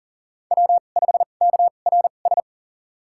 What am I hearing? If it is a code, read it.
Morse code, heard as a keyed tone, about 40 wpm, spaced as for 29 wpm, 700 Hz. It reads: W5KRS
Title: USA Call Signs - 5 Characters - 2x Character Spacing - 40wpm